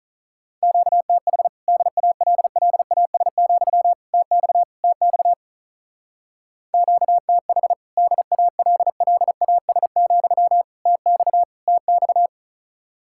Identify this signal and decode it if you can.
Morse code, heard as a keyed tone, about 44 wpm, 700 Hz. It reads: QTH DALLAS, TX TX QTH DALLAS, TX TX